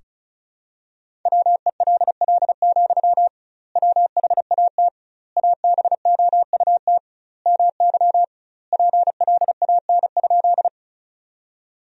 Morse code, heard as a keyed tone, about 36 wpm, 700 Hz. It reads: WELL, WHAT ABOUT MY PLAN?
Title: 2x Word-Spacing - Sentences from Top 500 Words - 35wpm